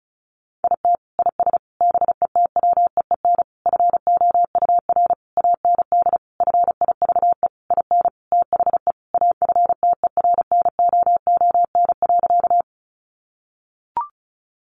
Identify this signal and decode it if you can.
Morse code, heard as a keyed tone, about 34 wpm, 700 Hz. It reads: IT IS BETWEEN FOUR AND FIVE IN THE AFTERNOON.
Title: Sentences from Top 1500 Words 35wpm